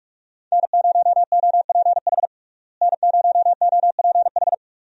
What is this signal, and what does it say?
Morse code, heard as a keyed tone, about 46 wpm, 700 Hz. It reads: N0OPH N0OPH
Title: USA Call Signs - 5 Characters - Repeated 3x - 45wpm